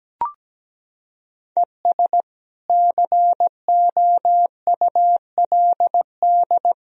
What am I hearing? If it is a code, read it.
Morse code, heard as a keyed tone, about 17 wpm, 700 Hz. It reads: ES COULD
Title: Sentences from Top 100 Words and Abbreviations 17wpm